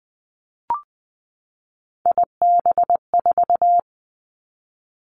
Morse code, heard as a keyed tone, about 20 wpm, 700 Hz. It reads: IB4
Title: Three Challenging Letter-Number Combinations 20wpm